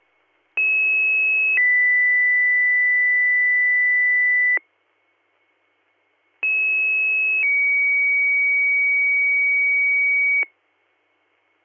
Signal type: two-tone page

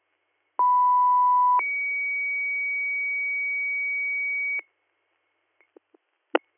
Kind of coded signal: two-tone page

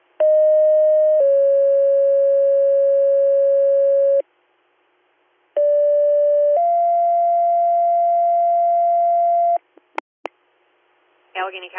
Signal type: two-tone page

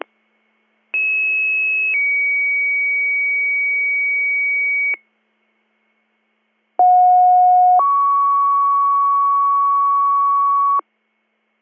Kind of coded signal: two-tone page

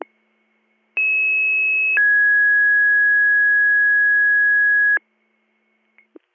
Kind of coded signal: two-tone page

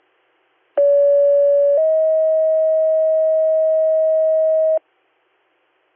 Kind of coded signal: two-tone page

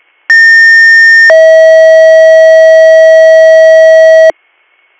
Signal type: two-tone page

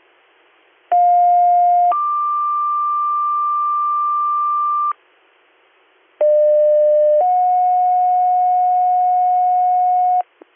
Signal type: two-tone page

